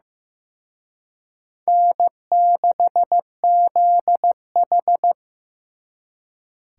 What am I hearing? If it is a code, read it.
Morse code, heard as a keyed tone, about 18 wpm, 700 Hz. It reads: N6ZH
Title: Speed Racing - USA 1x2 Call Signs 15wpm